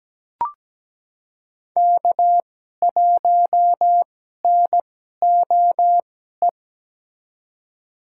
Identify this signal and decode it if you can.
Morse code, heard as a keyed tone, about 17 wpm, spaced as for 12 wpm, 700 Hz. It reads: K1NOE
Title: USA Call Signs - 5 Characters - 2x Character Spacing - 17wpm